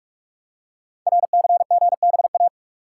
Morse code, heard as a keyed tone, about 44 wpm, 700 Hz. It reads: RCGBA
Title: Code Groups - 5 Letters - 45wpm